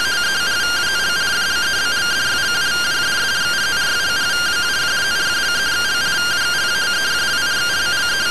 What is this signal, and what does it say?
Two-tone teleprinter signal, about 170 Hz shift.